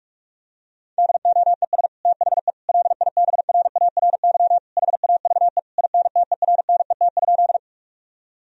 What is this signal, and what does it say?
Morse code, heard as a keyed tone, about 41 wpm, 700 Hz. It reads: DOES THE LIBRARY HAVE INTERNET?
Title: Sentences from Top 1000 Words - Review Part 4 - 45wpm